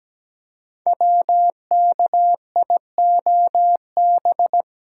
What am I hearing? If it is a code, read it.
Morse code, heard as a keyed tone, about 17 wpm, 700 Hz. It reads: WKIOB
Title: Code Groups - 5 Letters - 17wpm